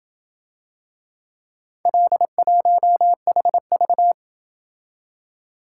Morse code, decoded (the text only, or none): L1HV